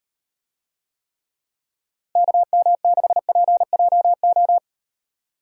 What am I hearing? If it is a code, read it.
Morse code, heard as a keyed tone, about 38 wpm, 700 Hz. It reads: KM6PJO